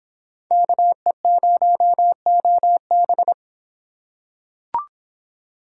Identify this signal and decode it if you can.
Morse code, heard as a keyed tone, about 26 wpm, 700 Hz. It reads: KE0OB